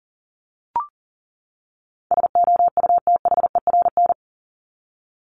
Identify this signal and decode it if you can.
Morse code, heard as a keyed tone, about 40 wpm, 700 Hz. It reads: SOUTHERN
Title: Top 1751-2000 Words 40wpm